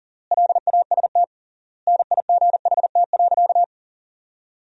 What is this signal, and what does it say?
Morse code, heard as a keyed tone, about 40 wpm, 700 Hz. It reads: LAST NIGHT.
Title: 3x Word-Spacing - Sentences from Top 300 Words - 40wpm